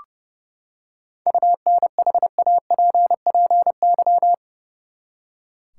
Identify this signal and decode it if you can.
Morse code, heard as a keyed tone, about 30 wpm, 700 Hz. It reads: UNHAPPY
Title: Seven Letter Words - 30wpm